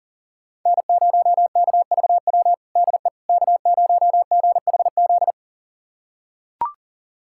Morse code, heard as a keyed tone, about 40 wpm, 690 Hz. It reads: N0KVW DE K0GHZ
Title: USA Callsign DE Callsign - 5 Characters - 40wpm